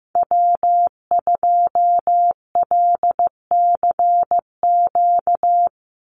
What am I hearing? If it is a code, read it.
Morse code, heard as a keyed tone, about 15 wpm, 700 Hz. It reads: W2LCQ
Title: Call Signs Repeated 2x - 15wpm